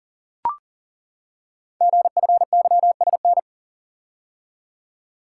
Morse code, heard as a keyed tone, about 42 wpm, 700 Hz. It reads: GFYSN
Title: Code Groups - 5 Letter:Numbers - 40wpm